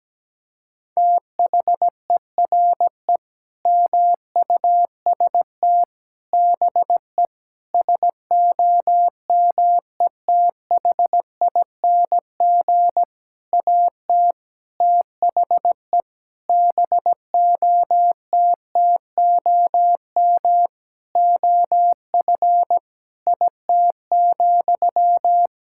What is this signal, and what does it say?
Morse code, heard as a keyed tone, about 17 wpm, 700 Hz. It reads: THERE MUST BE SOMETHING AT THE BOTTOM OF IT,
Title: Sentences from Top 1000 Words - Review Part 6 - 17wpm